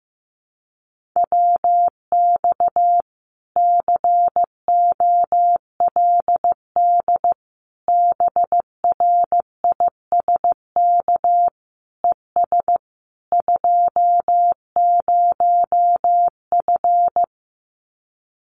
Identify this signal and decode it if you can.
Morse code, heard as a keyed tone, about 15 wpm, 700 Hz. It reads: WX COLD BRISK ES 20F
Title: Common QSO Phrases Encore 15wpm